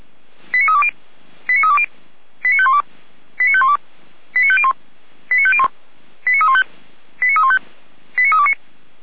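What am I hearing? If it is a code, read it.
Five-tone selective call (ZVEI), heard as quick runs of five tones, tones of about 70 ms. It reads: 89320 89320 89512 89512 89501 89501 89315 89315 89329
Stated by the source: SDR-received ZVEI five-tone selective call, address 89109.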